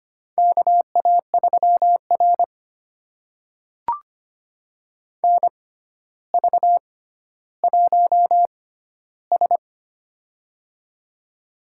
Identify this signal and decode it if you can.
Morse code, heard as a keyed tone, about 25 wpm, spaced as for 13 wpm, 700 Hz. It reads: KA3R NV1S
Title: USA Call Signs - 4 Characters - 6x Character Spacing - 25wpm